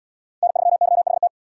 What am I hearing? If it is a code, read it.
Morse code, heard as a keyed tone, about 69 wpm, 700 Hz. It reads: N3YHT